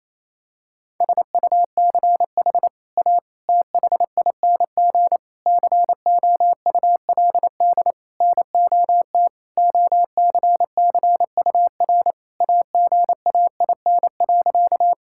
Morse code, heard as a keyed tone, about 28 wpm, 700 Hz. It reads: SUCH A THING COULD NOT OCCUR AGAIN.
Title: Sentences from Top 1000 Words - Review Part 6 - 28wpm